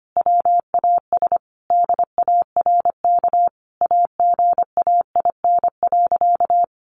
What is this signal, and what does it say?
Morse code, heard as a keyed tone, about 25 wpm, 700 Hz. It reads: WAS DARK AGAIN.
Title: Sentences from Top 500 Words 25wpm